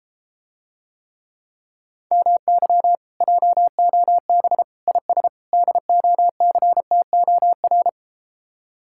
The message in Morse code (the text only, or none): MY JOB IS DOCTOR